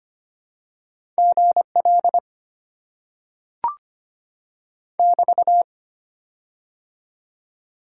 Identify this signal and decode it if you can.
Morse code, heard as a keyed tone, about 25 wpm, 690 Hz. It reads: GL <BT>
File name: Top 90 QSO Elements - 6x Character Spacing - 25wpm